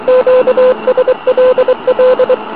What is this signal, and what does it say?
Morse code, heard as a keyed tone, about 23 wpm, 510 Hz, with noise. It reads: QSLL